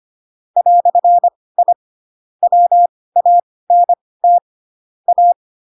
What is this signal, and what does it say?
Morse code, heard as a keyed tone, about 25 wpm, spaced as for 18 wpm, 700 Hz. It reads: "I WANT A
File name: Pinocchio / Pinocchio ch2 12 25 ik1hge